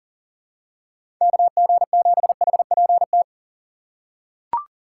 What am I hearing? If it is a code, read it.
Morse code, heard as a keyed tone, about 40 wpm, 700 Hz. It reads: KG7HPT
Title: USA Call Signs - 3x Character Spacing - 40wpm